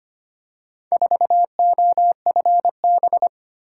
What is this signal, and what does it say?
Morse code, heard as a keyed tone, about 25 wpm, 700 Hz. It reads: 4OFB
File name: practice / 25wpm / Four Letter-Number Combinations - 25wpm